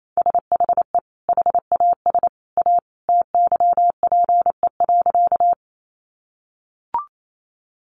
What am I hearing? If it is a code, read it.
Morse code, heard as a keyed tone, about 28 wpm, 700 Hz. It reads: SHE HAS A TYPE.
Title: Sentences from Top 600 Words 28wpm